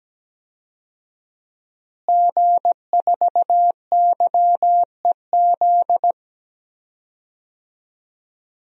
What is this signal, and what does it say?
Morse code, heard as a keyed tone, about 17 wpm, 700 Hz. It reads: G4YEZ
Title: DXCC-100 Call Signs - 5 Characters - Repeated 3x - 17wpm